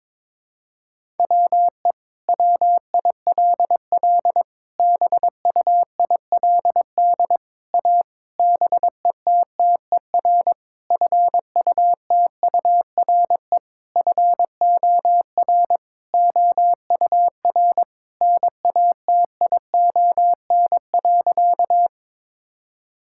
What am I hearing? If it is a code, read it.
Morse code, heard as a keyed tone, about 22 wpm, 700 Hz. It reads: WE WILL BUILD A BETTER FUTURE FOR OUR NATION.
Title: Sentences from Top 1000 Words - Review Part 6 - 22wpm